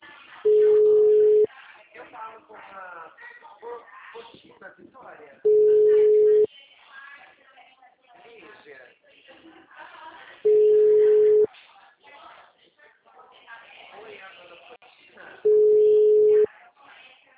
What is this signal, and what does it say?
Ringback tone (Europe, 425 Hz), with speech.